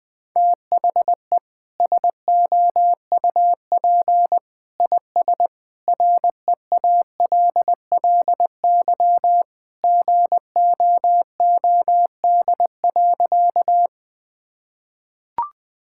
Morse code, heard as a keyed tone, about 20 wpm, 700 Hz. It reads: THE SOUP IS REALLY GOOD.
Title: Sentences from Top 1000 Words - Review Part 6 - 20wpm